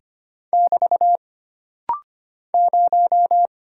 Morse code, heard as a keyed tone, about 25 wpm, 700 Hz. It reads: <BT> 0
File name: Single Character (Letter:Number:Punctuation:Prosign) - Rapid-Fire - 25wpm